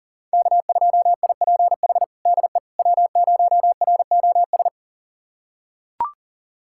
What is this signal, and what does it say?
Morse code, heard as a keyed tone, about 40 wpm, 690 Hz. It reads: K2IPH DE W0ROS
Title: USA Callsign DE Callsign - 5 Characters - 40wpm